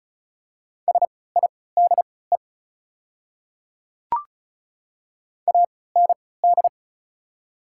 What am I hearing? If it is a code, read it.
Morse code, heard as a keyed tone, about 34 wpm, spaced as for 20 wpm, 700 Hz. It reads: SIDE AND
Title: 3x Character Spacing - Top 100 Words - 35wpm